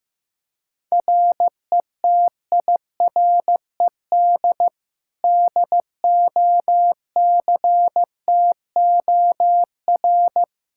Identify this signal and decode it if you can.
Morse code, heard as a keyed tone, about 15 wpm, 700 Hz. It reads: RETIRED DOCTOR
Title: Practice QSO - Exchange 3 - More Info - 15wpm